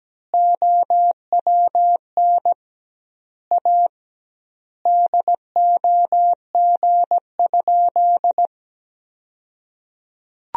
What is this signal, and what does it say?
Morse code, heard as a keyed tone, about 17 wpm, 700 Hz. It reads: OWN A DOG?
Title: 2x Word-Spacing - Sentences from Top 400 Words - 17wpm